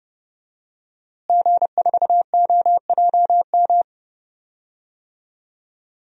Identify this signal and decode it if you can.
Morse code, heard as a keyed tone, about 30 wpm, 700 Hz. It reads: G4OJM
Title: DXCC-100 Call Signs - 5 Characters - Repeated 3x - 30wpm